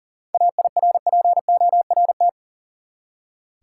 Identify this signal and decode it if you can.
Morse code, heard as a keyed tone, about 40 wpm, 700 Hz. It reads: AIRPORT
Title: Seven Letter Words - 40wpm